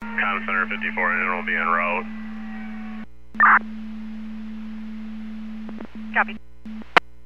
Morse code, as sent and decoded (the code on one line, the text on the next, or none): none